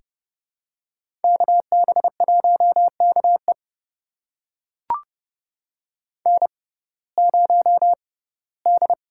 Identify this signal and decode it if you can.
Morse code, heard as a keyed tone, about 30 wpm, spaced as for 18 wpm, 700 Hz. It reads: KB1KE N0D
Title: USA Call Signs - 5 Characters - 6x Character Spacing - 30wpm